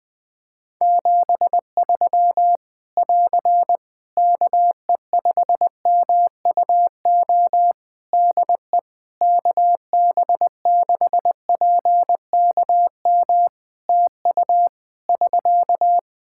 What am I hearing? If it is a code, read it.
Morse code, heard as a keyed tone, about 20 wpm, 700 Hz. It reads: 73 <AR> KE5MUO DE KB6PKM TU <SK>
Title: Practice QSO - Exchange 4 - Ending - 20wpm